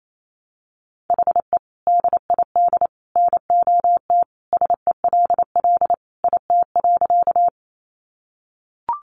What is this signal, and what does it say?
Morse code, heard as a keyed tone, about 28 wpm, 700 Hz. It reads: HE DID NOT SELL IT.
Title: Sentences from Top 1100 Words 28wpm